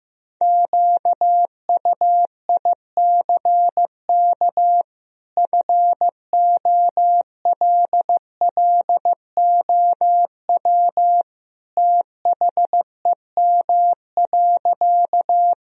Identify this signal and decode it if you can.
Morse code, heard as a keyed tone, about 15 wpm, 700 Hz. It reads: QUICK FOLLOW THEM.
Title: Sentences from Top 1000 Words - Review Part 1 - 15wpm